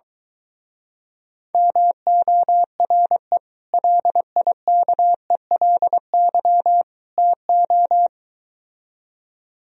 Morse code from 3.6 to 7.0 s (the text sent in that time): LIKELY